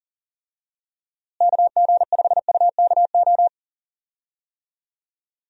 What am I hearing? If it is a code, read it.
Morse code, heard as a keyed tone, about 40 wpm, 700 Hz. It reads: KG5UKO
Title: USA Call Signs - 6 Characters - Repeated 3x - 40wpm